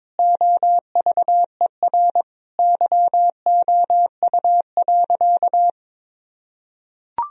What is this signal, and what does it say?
Morse code, heard as a keyed tone, about 22 wpm, 700 Hz. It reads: OVER YOU.
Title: Sentences from Top 1000 Words - Review Part 1 - 22wpm